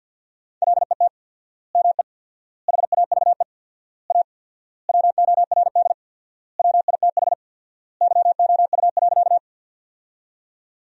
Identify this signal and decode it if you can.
Morse code, heard as a keyed tone, about 50 wpm, 700 Hz. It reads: LET ME HAVE A WORD WITH YOU.
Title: 4x Word-Spacing - Sentences from Top 300 Words - 50wpm